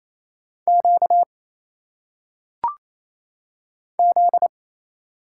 Morse code, heard as a keyed tone, about 27 wpm, 700 Hz. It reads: Q Z
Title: Single Letters - Phonetic Pronunciation 28wpm